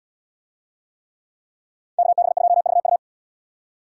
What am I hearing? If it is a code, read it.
Morse code, heard as a keyed tone, about 72 wpm, 700 Hz. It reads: KB2VU